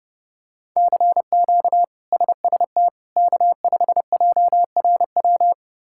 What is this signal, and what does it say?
Morse code, heard as a keyed tone, about 30 wpm, 700 Hz. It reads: CQ SST K5JRW
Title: K1USN SST QSO Practice - 30wpm